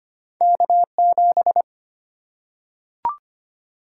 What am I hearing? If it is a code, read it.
Morse code, heard as a keyed tone, about 25 wpm, 700 Hz. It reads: K7
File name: Two Letter-Number Combinations - 25wpm